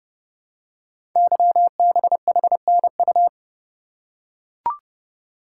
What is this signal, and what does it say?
Morse code, heard as a keyed tone, about 28 wpm, 700 Hz. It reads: YBHNU
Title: Code Groups - 5 Letter:Numbers - 30wpm